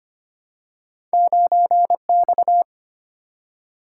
Morse code, heard as a keyed tone, about 25 wpm, 700 Hz. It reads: 9X
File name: Two Letter-Number Combinations - 25wpm